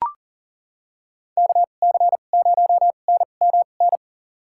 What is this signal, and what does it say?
Morse code, heard as a keyed tone, about 40 wpm, spaced as for 29 wpm, 700 Hz. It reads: KC0NMN
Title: USA Call Signs - 6 Characters - 2x Character Spacing - 40wpm